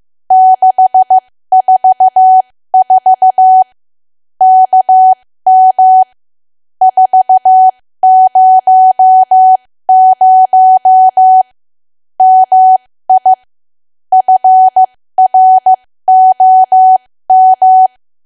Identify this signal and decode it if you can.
Morse code, heard as a keyed tone, about 15 wpm, spaced as for 13 wpm, 750 Hz. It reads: 644 KM 400 MI FROM
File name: cwa / W1AW-Level2 / 13WPM